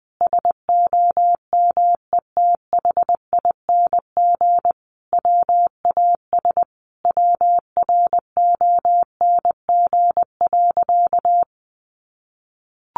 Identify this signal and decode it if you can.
Morse code, heard as a keyed tone, about 20 wpm, 700 Hz. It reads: SOMETHING WAS WRONG.
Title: Sentences from Top 1200 Words 20wpm